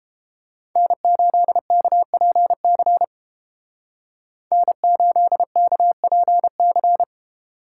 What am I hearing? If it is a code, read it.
Morse code, heard as a keyed tone, about 31 wpm, 700 Hz. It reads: N8KPC N8KPC